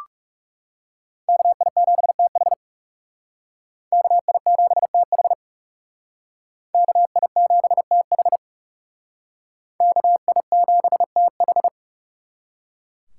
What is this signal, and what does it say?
Morse code, heard as a keyed tone, about 37 wpm, 700 Hz. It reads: KI7TH KI7TH KI7TH KI7TH
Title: Speed Racing - USA Call Signs 30wpm